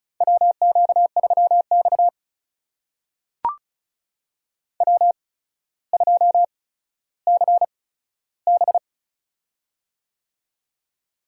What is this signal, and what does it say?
Morse code, heard as a keyed tone, about 34 wpm, spaced as for 15 wpm, 700 Hz. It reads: WQ3X W2CB
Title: USA Call Signs - 4 Characters - 8x Character Spacing - 35wpm